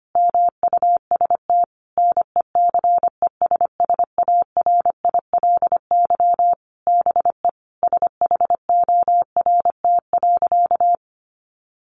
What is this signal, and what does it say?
Morse code, heard as a keyed tone, about 25 wpm, 700 Hz. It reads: MUST NECESSARILY BE SHORT.
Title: Sentences from Top 1400 Words 25wpm